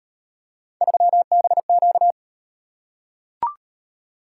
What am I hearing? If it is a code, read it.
Morse code, heard as a keyed tone, about 38 wpm, 700 Hz. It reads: 3BQ